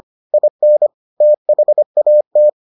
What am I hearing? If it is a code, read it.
Morse code, heard as a keyed tone, about 25 wpm, 600 Hz. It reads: IN THAT